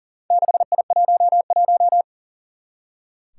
Morse code, heard as a keyed tone, about 40 wpm, 700 Hz. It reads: 6I11
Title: Four Challenging Letter-Number Combinations 40wpm